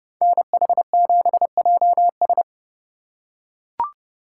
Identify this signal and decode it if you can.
Morse code, heard as a keyed tone, about 29 wpm, 700 Hz. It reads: NH7JS